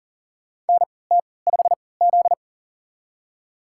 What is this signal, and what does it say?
Morse code, heard as a keyed tone, about 40 wpm, spaced as for 23 wpm, 700 Hz. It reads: NT5Z
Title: USA Call Signs - 4 Characters - 3x Character Spacing - 40wpm